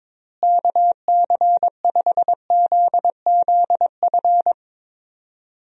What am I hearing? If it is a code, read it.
Morse code, heard as a keyed tone, about 22 wpm, 700 Hz. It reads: KC5ZZF
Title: USA Call Signs - 6 Characters - Repeated 3x - 22wpm